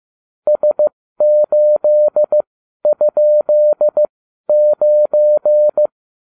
Morse code, heard as a keyed tone, about 15 wpm, spaced as for 12 wpm, 600 Hz. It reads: S8?9